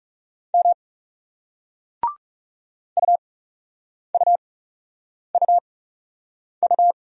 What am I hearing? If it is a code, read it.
Morse code, heard as a keyed tone, about 38 wpm, spaced as for 8 wpm, 700 Hz. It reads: M UUUU